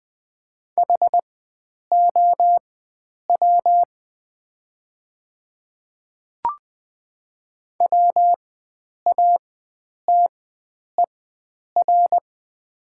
Morse code, heard as a keyed tone, about 20 wpm, spaced as for 9 wpm, 690 Hz. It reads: HOW WATER
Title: 4x Character Spacing - Top 100 Words - 20wpm